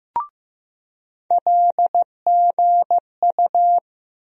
Morse code, heard as a keyed tone, about 15 wpm, 700 Hz. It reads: LGU
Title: USA Call Sign Postfixes - 3 Letters 15wpm